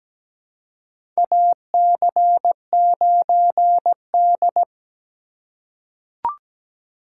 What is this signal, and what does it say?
Morse code, heard as a keyed tone, about 17 wpm, 700 Hz. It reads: AC9D
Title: USA Call Signs - 4 Characters - 4x Character Spacing - 17wpm